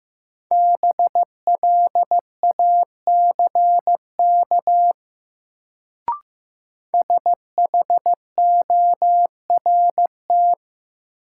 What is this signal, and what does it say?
Morse code, heard as a keyed tone, about 15 wpm, 700 Hz. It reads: BLACK SHORT
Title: Sentences from Top 400 Words - New Word Drill - 15wpm